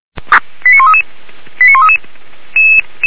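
Five-tone selective call (ZVEI), heard as quick runs of five tones, tones of about 70 ms. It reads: 89133 89133
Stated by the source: Five-tone ZVEI selective call 89110, received by SDR.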